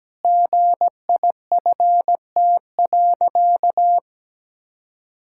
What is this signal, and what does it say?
Morse code, heard as a keyed tone, about 17 wpm, 700 Hz. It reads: GIFT.
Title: Sentences from Top 1000 Words - Review Part 6 - 17wpm